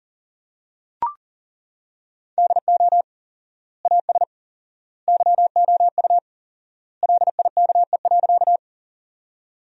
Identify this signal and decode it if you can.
Morse code, heard as a keyed tone, about 40 wpm, 700 Hz. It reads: DO AS YOU LIKE.
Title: 4x Word-Spacing - Sentences from Top 100 Words - 40wpm